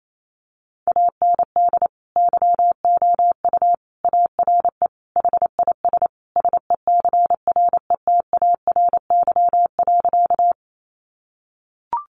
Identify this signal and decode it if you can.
Morse code, heard as a keyed tone, about 28 wpm, 700 Hz. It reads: AND YOU ARE HIS SECRETARY.
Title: Sentences from Top 1300 Words 28wpm